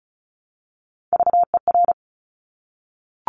Morse code, heard as a keyed tone, about 36 wpm, 700 Hz. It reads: VER
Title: Top Three Letter Combinations in Words 35wpm